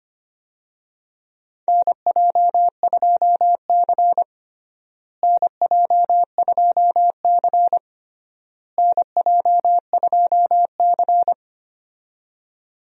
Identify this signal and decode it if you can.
Morse code, heard as a keyed tone, about 25 wpm, 700 Hz. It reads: NJ2C NJ2C NJ2C
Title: USA Call Signs - 4 Characters - Repeated 3x - 25wpm